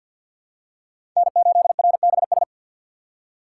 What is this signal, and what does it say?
Morse code, heard as a keyed tone, about 53 wpm, 700 Hz. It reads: N8RBS
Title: USA Call Signs - 6x Character Spacing - 50wpm